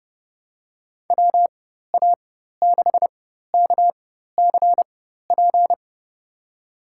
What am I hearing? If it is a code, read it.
Morse code, heard as a keyed tone, about 30 wpm, spaced as for 14 wpm, 700 Hz. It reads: WA6KCP
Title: USA Call Signs - 6 Characters - 4x Character Spacing - 30wpm